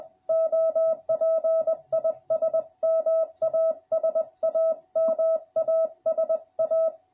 Morse code, heard as a keyed tone, about 20 wpm, 650 Hz. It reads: OP IS MASA MASA